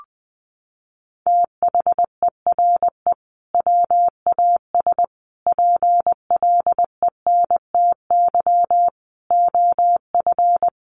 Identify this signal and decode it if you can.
Morse code, heard as a keyed tone, about 19 wpm, 700 Hz. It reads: THERE WAS PLENTY OF
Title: Sentences from Top 1300 Words 20wpm